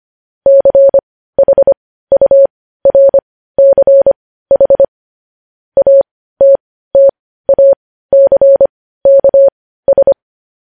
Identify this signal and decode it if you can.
Morse code, heard as a keyed tone, about 25 wpm, spaced as for 15 wpm, 550 Hz. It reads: CHURCH ATTACKS
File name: podcast / News headlines-1x-25x15